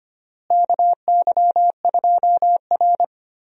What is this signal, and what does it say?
Morse code, heard as a keyed tone, about 25 wpm, 700 Hz. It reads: KY2R
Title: USA Call Signs - 4 Characters - Repeated 3x - 25wpm